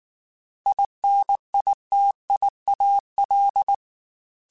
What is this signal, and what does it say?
Morse code, heard as a keyed tone, about 19 wpm, 800 Hz. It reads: INITIAL
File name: Technical Subjects / Technical Subjects-mp3s / Gates Intro @18